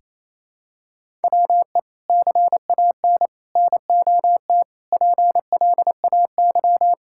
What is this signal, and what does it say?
Morse code, heard as a keyed tone, about 28 wpm, 700 Hz. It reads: WE CAN NOT PLAY
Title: Sentences from Top 1000 Words - Review Part 6 - 28wpm